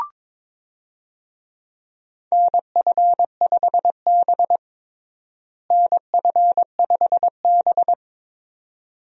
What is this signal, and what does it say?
Morse code, heard as a keyed tone, about 22 wpm, 700 Hz. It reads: NF5B NF5B
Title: USA Call Signs - 4 Characters - Repeated 2x - 22wpm